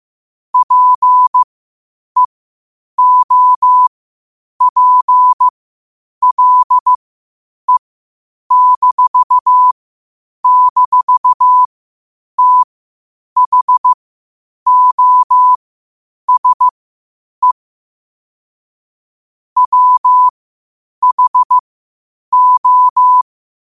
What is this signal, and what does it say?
Morse code, heard as a keyed tone, about 15 wpm, spaced as for 8 wpm, 1000 Hz. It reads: PEOPLE--THOSE WHO
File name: qotdfiles / 8wpm